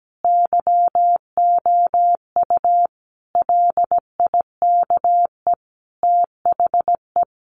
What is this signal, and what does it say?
Morse code, heard as a keyed tone, about 17 wpm, 700 Hz. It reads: YOU LIKE THE
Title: Sentences from Top 1200 Words 17wpm